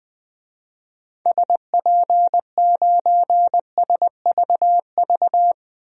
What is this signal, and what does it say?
Morse code, heard as a keyed tone, about 20 wpm, 700 Hz. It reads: SP9SVV
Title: DXCC-100 Call Signs - 6 Characters - 20wpm